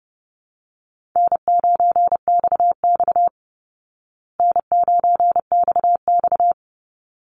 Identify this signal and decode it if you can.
Morse code, heard as a keyed tone, about 30 wpm, 700 Hz. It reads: N9XX N9XX
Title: Easy Call Signs Repeated 2x - 30wpm